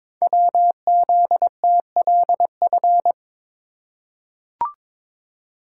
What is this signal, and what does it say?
Morse code, heard as a keyed tone, about 22 wpm, 690 Hz. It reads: WZTLF
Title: Code Groups - 5 Letters - 22wpm